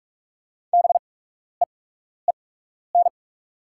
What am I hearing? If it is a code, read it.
Morse code, heard as a keyed tone, about 44 wpm, spaced as for 12 wpm, 690 Hz. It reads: BEEN
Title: 8x Character Spacing - Top 100 Words - 45wpm